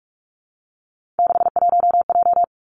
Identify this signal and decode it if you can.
Morse code, heard as a keyed tone, about 46 wpm, 700 Hz. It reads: 61J
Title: Three Challenging Letter-Number Combinations 45wpm